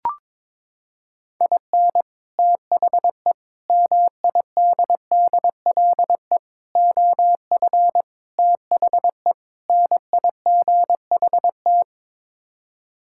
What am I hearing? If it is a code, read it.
Morse code, heard as a keyed tone, about 22 wpm, 700 Hz. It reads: IN THE MIDDLE OF THE NIGHT
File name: practice / 22wpm / Sets of 6 Words - 22wpm